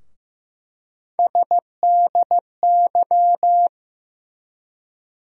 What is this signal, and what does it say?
Morse code, heard as a keyed tone, about 15 wpm, 700 Hz. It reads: SDY
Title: USA Call Sign Postfixes - 3 Letters 15wpm